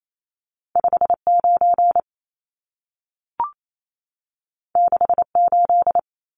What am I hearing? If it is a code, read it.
Morse code, heard as a keyed tone, about 28 wpm, 700 Hz. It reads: 59 68
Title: Numbers - 2-Digit 28wpm